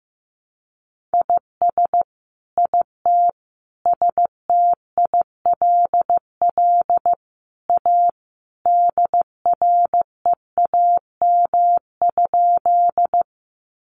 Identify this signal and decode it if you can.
Morse code, heard as a keyed tone, about 15 wpm, 700 Hz. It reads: IS IT STILL A DREAM?
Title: Sentences from Top 1500 Words 15wpm